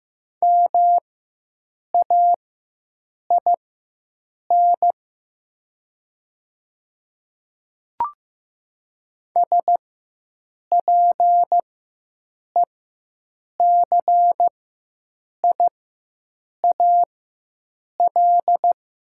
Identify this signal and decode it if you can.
Morse code, heard as a keyed tone, about 15 wpm, spaced as for 7 wpm, 700 Hz. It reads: MAIN SPECIAL
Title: 4x Character Spacing - Top 500 Words - 15wpm